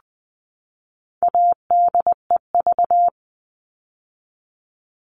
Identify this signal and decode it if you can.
Morse code, heard as a keyed tone, about 20 wpm, 700 Hz. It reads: ADEV
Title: All Cut Numbers - 4-Digit 20wpm